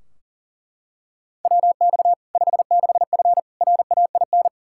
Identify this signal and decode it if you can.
Morse code, heard as a keyed tone, about 40 wpm, 700 Hz. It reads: WX 56F RAIN
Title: Practice QSO - Exchange 2 - Info - 40wpm